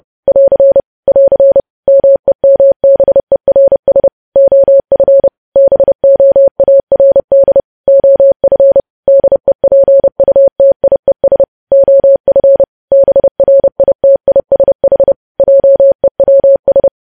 Morse code, heard as a keyed tone, about 30 wpm, 550 Hz. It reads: <AR> <AR> MEMBERS OF BOARD OF DEPUTIES OF BRITISH JEWS
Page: news headlines in morse code at 30 wpm